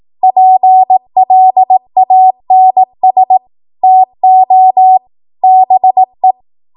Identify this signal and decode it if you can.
Morse code, heard as a keyed tone, about 18 wpm, 750 Hz. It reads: PLANS TO BE
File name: Archive / 18 WPM / 18WPM